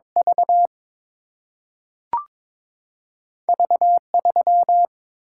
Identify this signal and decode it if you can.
Morse code, heard as a keyed tone, about 22 wpm, 700 Hz. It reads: V V 3
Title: Single-Letter Single-Number 22wpm